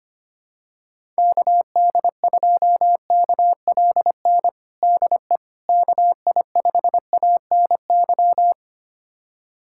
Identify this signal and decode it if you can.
Morse code, heard as a keyed tone, about 25 wpm, 700 Hz. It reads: KD2KLN DE KI5ANY
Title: USA Callsign DE Callsign - 6 Characters - 25wpm